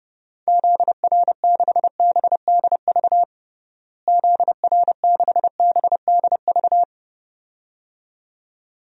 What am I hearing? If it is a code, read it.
Morse code, heard as a keyed tone, about 30 wpm, 700 Hz. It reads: ZR6BDV ZR6BDV
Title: DXCC-100 Call Signs - 6 Characters - Repeated 2x - 30wpm